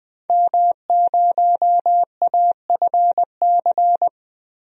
Morse code, heard as a keyed tone, about 20 wpm, 700 Hz. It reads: M0AFC
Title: DXCC-100 Call Signs - 5 Characters - 20wpm